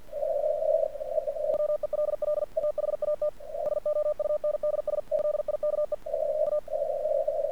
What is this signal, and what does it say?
Morse code, heard as a keyed tone, about 50 wpm, 1200 Hz. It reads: WELL THAT SOUNDS LIKE A
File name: archived / CRSnet